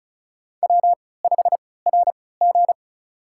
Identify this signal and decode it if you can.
Morse code, heard as a keyed tone, about 36 wpm, spaced as for 20 wpm, 700 Hz. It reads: W5RG
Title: USA Call Signs - 4 Characters - 3x Character Spacing - 35wpm